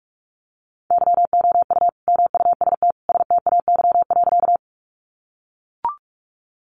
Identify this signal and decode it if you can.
Morse code, heard as a keyed tone, about 44 wpm, 700 Hz. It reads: YOU MUST STAY.
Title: Sentences from Top 500 Words 45wpm